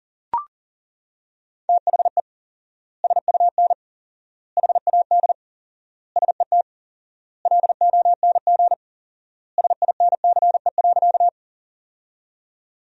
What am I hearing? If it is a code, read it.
Morse code, heard as a keyed tone, about 40 wpm, 700 Hz. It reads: THE SUN HAD SET LONG SINCE.
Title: 4x Word-Spacing - Sentences from Top 300 Words - 40wpm